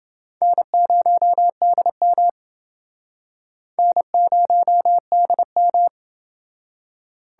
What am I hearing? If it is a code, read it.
Morse code, heard as a keyed tone, about 30 wpm, 700 Hz. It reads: N0DM N0DM